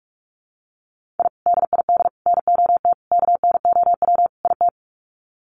Morse code, heard as a keyed tone, about 44 wpm, 690 Hz. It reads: I DID NOT KNOW IT
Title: Sets of 5 Words Encore2 45wpm